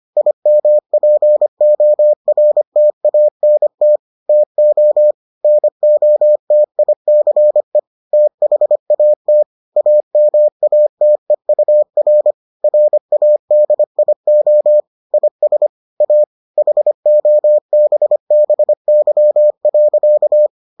Morse code, heard as a keyed tone, about 25 wpm, 600 Hz. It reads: IMPORTANT TO NOTICE THAT AMATEUR RADIO IS A HOBBY.